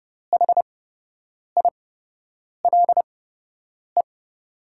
Morse code, heard as a keyed tone, about 30 wpm, spaced as for 8 wpm, 700 Hz. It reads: HILE
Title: 8x Character Spacing - Top 300 Words - 30wpm